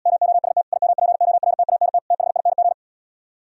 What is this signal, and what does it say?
Morse code, heard as a keyed tone, about 77 wpm, 700 Hz. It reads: MOST IMPORTANT THING